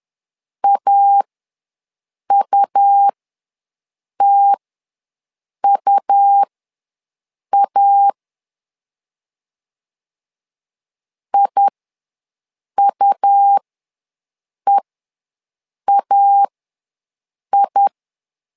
Morse code, heard as a keyed tone, about 11 wpm, spaced as for 6 wpm, 800 Hz. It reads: AUTUA IUEAI
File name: LEZIONI / 2 UHDN / 2 U ws 10wpm 2017